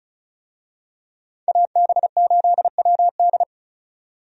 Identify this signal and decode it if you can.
Morse code, heard as a keyed tone, about 34 wpm, 700 Hz. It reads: AB8WD
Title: USA Call Signs - 5 Characters - 35wpm